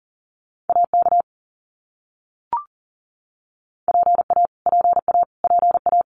Morse code, heard as a keyed tone, about 40 wpm, 700 Hz. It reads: AK PA PA PA
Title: US States Repeated 3x 40wpm